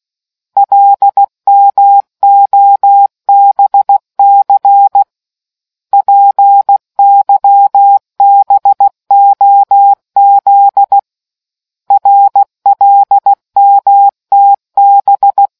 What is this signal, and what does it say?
Morse code, heard as a keyed tone, about 16 wpm, 800 Hz. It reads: LMOBC PYBOZ RLMTB